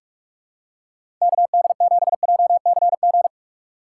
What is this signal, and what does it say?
Morse code, heard as a keyed tone, about 43 wpm, 700 Hz. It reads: KD7JCG
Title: USA Call Signs - 6 Characters - 3x Character Spacing - 45wpm